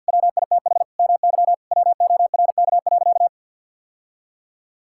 Morse code, heard as a keyed tone, about 46 wpm, 700 Hz. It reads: WITH MY WORK.
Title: Sentences from Top 1000 Words - Review Part 6 - 50wpm